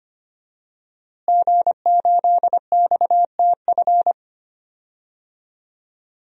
Morse code, heard as a keyed tone, about 25 wpm, 700 Hz. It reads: G8XTF
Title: DXCC-100 Call Signs - 5 Characters - Repeated 3x - 25wpm